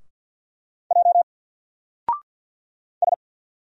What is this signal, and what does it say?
Morse code, heard as a keyed tone, about 50 wpm, 700 Hz. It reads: J S
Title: Single Letter-Number 50wpm